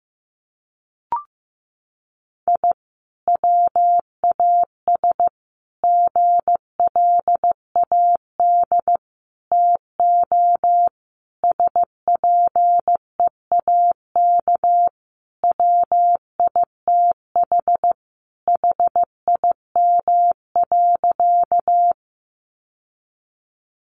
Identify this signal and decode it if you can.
Morse code, heard as a keyed tone, about 15 wpm, 700 Hz. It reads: I WAS GLAD TO SPEAK WITH HIM.
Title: Sentences from Top 1100 Words 15wpm